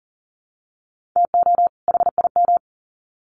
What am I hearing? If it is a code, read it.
Morse code, heard as a keyed tone, about 40 wpm, 700 Hz. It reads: TO HIM
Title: Sets of 2 Words 40wpm